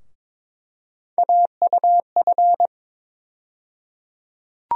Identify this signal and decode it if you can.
Morse code, heard as a keyed tone, about 21 wpm, 700 Hz. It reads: AUF
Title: USA Call Sign Postfixes - 3 Letters 22wpm